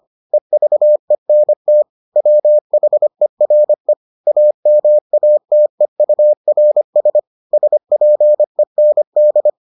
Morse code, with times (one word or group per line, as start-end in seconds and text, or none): 0.3-1.8 EVENT
2.2-3.9 WHERE
4.3-7.2 AMATEURS
7.5-9.5 SPEND